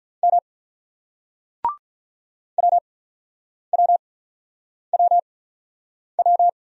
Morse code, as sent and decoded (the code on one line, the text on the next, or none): -- / .-- .-- .-- .--
M WWWW